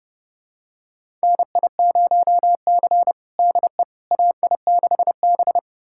Morse code, heard as a keyed tone, about 30 wpm, 700 Hz. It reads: NI0C DE AI6B
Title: USA Callsign DE Callsign - 4 Characters - 30wpm